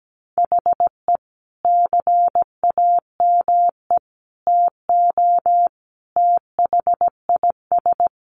Morse code, heard as a keyed tone, about 17 wpm, 700 Hz. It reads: HE CAME TO THIS
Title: A Sound Of Thunder 17wpm